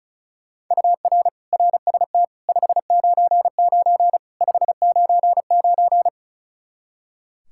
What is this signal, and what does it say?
Morse code, heard as a keyed tone, about 36 wpm, 700 Hz. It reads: UR RST 599 599
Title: Practice QSO - Exchange 1 - Beginning - 35wpm